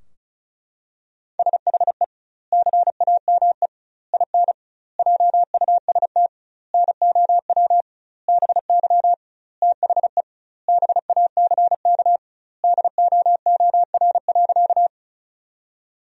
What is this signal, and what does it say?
Morse code, heard as a keyed tone, about 36 wpm, 700 Hz. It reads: SHE CAME IN JUST NOW BY THE BACK DOOR.
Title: 2x Word-Spacing - Sentences from Top 300 Words - 35wpm